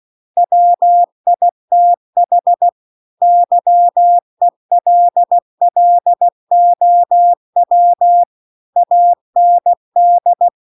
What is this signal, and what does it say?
Morse code, heard as a keyed tone, about 16 wpm, 700 Hz. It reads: WITH YELLOW AND